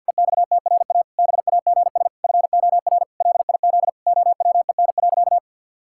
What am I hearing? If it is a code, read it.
Morse code, heard as a keyed tone, about 50 wpm, 700 Hz. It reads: EXTRA BAGS FOR LIZ OWEN.
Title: Pangram Challenge v2 - 50wpm